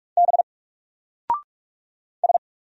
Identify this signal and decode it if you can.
Morse code, heard as a keyed tone, about 45 wpm, 700 Hz. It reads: B S
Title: Single Letter-Number 45wpm